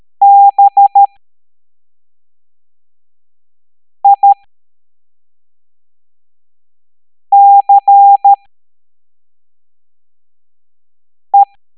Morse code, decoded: BICE